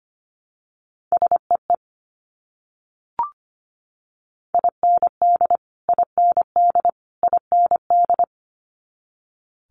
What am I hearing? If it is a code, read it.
Morse code, heard as a keyed tone, about 25 wpm, 700 Hz. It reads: SEE IND IND IND
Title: Multi-letter Phonemes Repeated 3x 25wpm